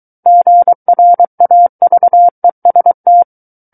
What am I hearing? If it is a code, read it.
Morse code, heard as a keyed tone, about 23 wpm, 700 Hz. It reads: GRAVEST
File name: Book One / 14-Chapter 14 - 23 WPM